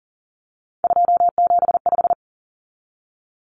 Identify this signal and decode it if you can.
Morse code, heard as a keyed tone, about 40 wpm, 700 Hz. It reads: 275
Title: Common Cut Numbers - 3-Digit 40wpm